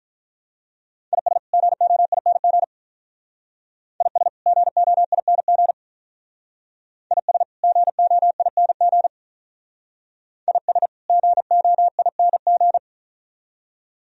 Morse code, heard as a keyed tone, about 43 wpm, 700 Hz. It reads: IS GOING IS GOING IS GOING IS GOING
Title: Speed-Racing - Sets of 2 Words 35wpm